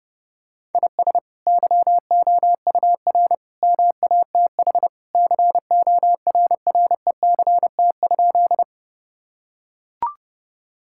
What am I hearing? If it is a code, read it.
Morse code, heard as a keyed tone, about 30 wpm, 700 Hz. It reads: IS YOUR MATH CORRECT?
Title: Sentences from Top 1000 Words - Review Part 6 - 30wpm